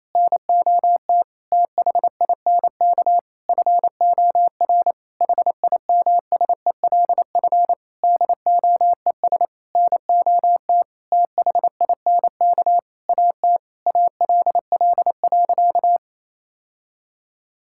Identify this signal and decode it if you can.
Morse code, heard as a keyed tone, about 28 wpm, 700 Hz. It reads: NOT THINK FOR HIMSELF DOES NOT THINK AT ALL.
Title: Sentences from Top 1600 Words 28wpm